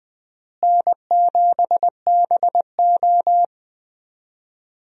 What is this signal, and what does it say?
Morse code, heard as a keyed tone, about 21 wpm, 700 Hz. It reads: N7BO
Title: USA Call Signs - 4 Characters - Repeated 2x - 20wpm